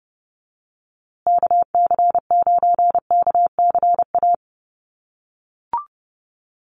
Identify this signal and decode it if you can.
Morse code, heard as a keyed tone, about 30 wpm, 700 Hz. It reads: KC9KCA
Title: USA Call Signs Repeated 3x 30wpm